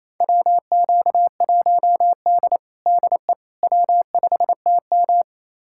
Morse code, heard as a keyed tone, about 28 wpm, 700 Hz. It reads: WQ1D DE W5TM
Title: USA Callsign DE Callsign - 4 Characters - 28wpm